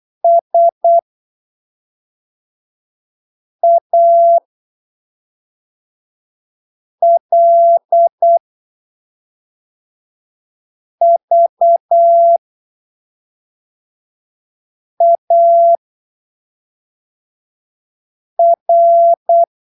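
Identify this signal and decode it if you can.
Morse code, heard as a keyed tone, about 8 wpm, spaced as for 3 wpm, 680 Hz. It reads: SALVAR
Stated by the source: Morse training text at 8 ppm for Brazilian class B amateur radio telegraphy exams, from Portal do CW, PY2BRL.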